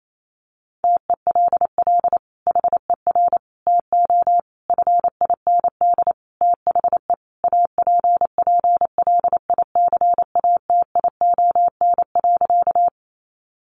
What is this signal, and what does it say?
Morse code, heard as a keyed tone, about 28 wpm, 700 Hz. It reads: TELL HER TO FIND THE APPLICATION.
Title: Sentences from Top 1200 Words 28wpm